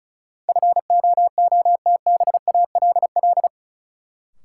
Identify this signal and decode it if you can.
Morse code, heard as a keyed tone, about 36 wpm, 700 Hz. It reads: FOOTBALL
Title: Sentences from Top 1600 Words - New Word Drill - 35wpm